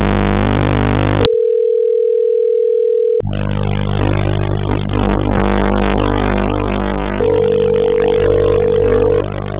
Telephone call progress tone (North America), heard as a telephone network's tone, about 440, 480 Hz, with noise. Ringback tone.